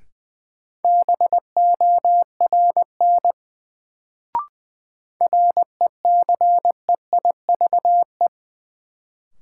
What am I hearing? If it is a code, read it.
Morse code, heard as a keyed tone, about 20 wpm, 700 Hz. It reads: BORN RECEIVE
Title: Sentences from Top 1000 Words - New Word Drill - 20wpm